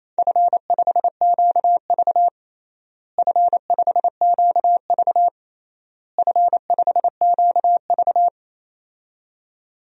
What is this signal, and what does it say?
Morse code, heard as a keyed tone, about 27 wpm, 700 Hz. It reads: F5QV F5QV F5QV
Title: DXCC-100 Call Signs - 4 Characters - Repeated 3x - 28wpm